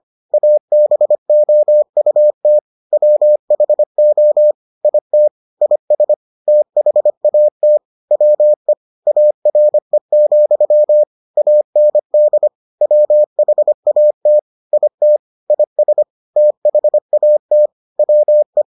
Morse code, heard as a keyed tone, about 25 wpm, 600 Hz. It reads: ABOUT WHO IT IS THAT WE ARE, AND WHAT IT IS THAT WE